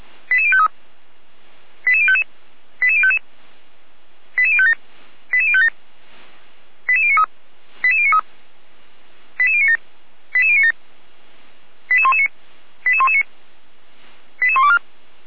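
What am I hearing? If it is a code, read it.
Five-tone selective call (ZVEI), heard as quick runs of five tones, tones of about 70 ms. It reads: 88053 88055 88055 88057 88057 88093 88093 88097 88097 88119 88119 88124
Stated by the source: ZVEI five-tone selective call received on an SDR, channel 455, by Feuerwehr Orlamünde.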